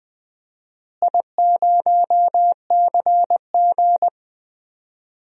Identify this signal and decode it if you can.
Morse code, heard as a keyed tone, about 20 wpm, 700 Hz. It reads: I0CG
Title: DXCC-100 Call Signs - 4 Characters - Repeated 2x - 20wpm